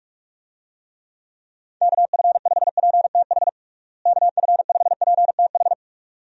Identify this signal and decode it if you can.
Morse code, heard as a keyed tone, about 46 wpm, 700 Hz. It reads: KF5PTH KF5PTH
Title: USA Call Signs - 6 Characters - Repeated 2x - 45wpm